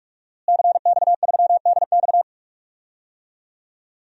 Morse code, heard as a keyed tone, about 46 wpm, 700 Hz. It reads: CX3DX
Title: DXCC-100 Call Signs - 5 Characters - Repeated 3x - 45wpm